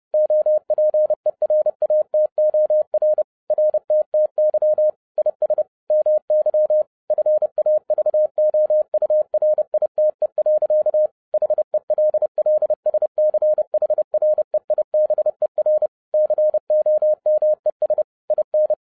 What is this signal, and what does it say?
Morse code, heard as a keyed tone, about 30 wpm, 610 Hz. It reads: OPERATOR RTTY IS MY FAVOURITE. HELLSCHREIBER COMES IN